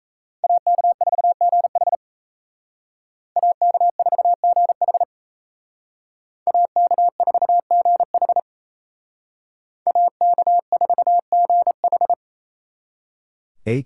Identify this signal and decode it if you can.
Morse code, heard as a keyed tone, about 35 wpm, 700 Hz. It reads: AK4GH AK4GH AK4GH AK4GH
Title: Speed Racing - USA 2x2 Call Signs 28wpm